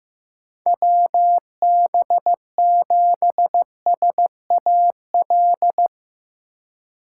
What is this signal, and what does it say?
Morse code, heard as a keyed tone, about 15 wpm, 700 Hz. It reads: WB7SAL
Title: USA Call Signs - 6 Characters - 2x Character Spacing - 15wpm